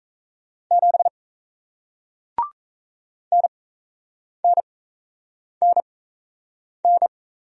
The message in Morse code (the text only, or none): 7 NNNN